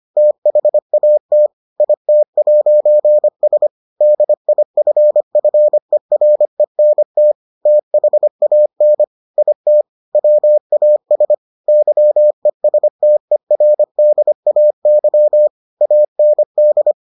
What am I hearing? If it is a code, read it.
Morse code, heard as a keyed tone, about 25 wpm, 600 Hz. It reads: THAT IT'S DIFFERENT THAN IT WAS YESTERDAY AND